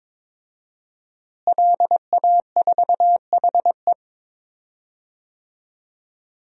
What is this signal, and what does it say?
Morse code, heard as a keyed tone, about 23 wpm, 700 Hz. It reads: LA4HE
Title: DXCC-100 Call Signs - 5 Characters - Repeated 3x - 22wpm